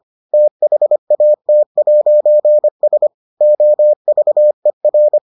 Morse code, heard as a keyed tone, about 25 wpm, 600 Hz. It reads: THAT'S OVER